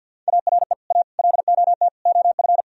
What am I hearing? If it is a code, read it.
Morse code, heard as a keyed tone, about 61 wpm, 700 Hz. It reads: ARE A LOT OF
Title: Speed-Racing - Sets of 4 Words 50wpm